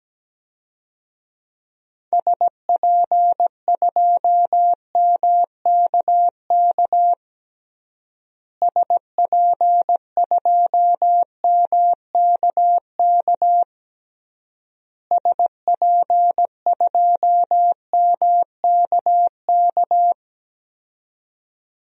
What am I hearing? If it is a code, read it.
Morse code, heard as a keyed tone, about 17 wpm, 700 Hz. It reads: SP2MKK SP2MKK SP2MKK
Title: DXCC-100 Call Signs - 6 Characters - Repeated 3x - 17wpm